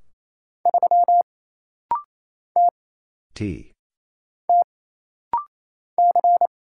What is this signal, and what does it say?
Morse code, heard as a keyed tone, about 29 wpm, 700 Hz. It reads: 3 T T C
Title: Single Letter-Number - Rapid-Fire 28wpm